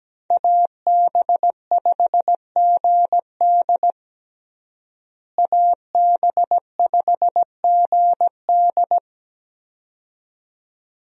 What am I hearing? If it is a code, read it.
Morse code, heard as a keyed tone, about 17 wpm, 700 Hz. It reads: AB5GD AB5GD
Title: USA Call Signs - 5 Characters - Repeated 2x - 17wpm